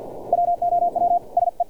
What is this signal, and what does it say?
Morse code, heard as a keyed tone, about 50 wpm, 700 Hz. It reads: WOW SE
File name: archived / CRSnet